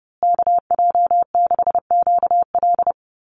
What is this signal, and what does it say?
Morse code, heard as a keyed tone, about 28 wpm, 700 Hz. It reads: KJ6QL
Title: USA Call Signs Repeated 3x 30wpm